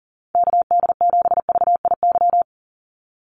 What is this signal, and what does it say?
Morse code, heard as a keyed tone, about 40 wpm, 700 Hz. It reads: KD7VIY
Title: USA Call Signs 40wpm